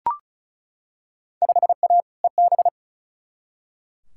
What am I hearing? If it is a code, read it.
Morse code, heard as a keyed tone, about 36 wpm, 690 Hz. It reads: 5A EB
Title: Field Day Exchanges Encore - 35wpm